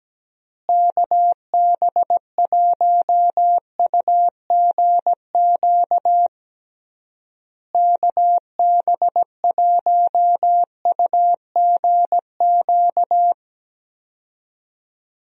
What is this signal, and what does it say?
Morse code, heard as a keyed tone, about 17 wpm, 700 Hz. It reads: KB1UGQ KB1UGQ
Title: USA Call Signs - 6 Characters - Repeated 3x - 17wpm